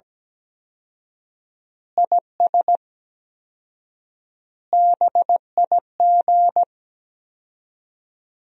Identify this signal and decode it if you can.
Morse code, heard as a keyed tone, about 17 wpm, 700 Hz. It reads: IS BIG
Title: 4x Word-Spacing - Sentences from Top 400 Words - 17wpm